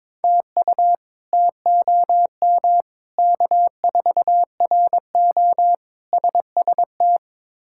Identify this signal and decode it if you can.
Morse code, heard as a keyed tone, about 22 wpm, 700 Hz. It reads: TU TOM K4RO SST
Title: K1USN SST QSO Practice - 22wpm